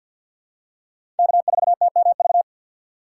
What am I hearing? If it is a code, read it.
Morse code, heard as a keyed tone, about 53 wpm, 700 Hz. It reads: K4TMV
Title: USA Call Signs - 6x Character Spacing - 50wpm